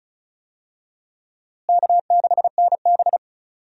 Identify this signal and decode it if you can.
Morse code, heard as a keyed tone, about 36 wpm, 700 Hz. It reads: K6NB
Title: USA Call Signs - 4 Characters - Repeated 2x - 35wpm